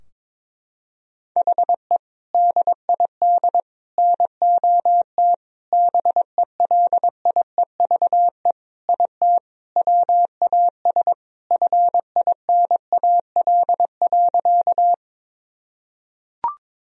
Morse code, heard as a keyed tone, about 22 wpm, 700 Hz. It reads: HE DID NOT BELIEVE IT WAS FINAL.
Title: Sentences from Top 1000 Words - Review Part 2 - 22wpm